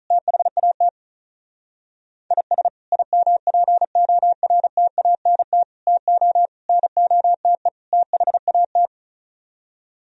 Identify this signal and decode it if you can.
Morse code, heard as a keyed tone, about 44 wpm, 700 Hz. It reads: THAT IS IMPORTANT TO NOTE THAT
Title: Speed-Racing - Sets of 5 Words 35wpm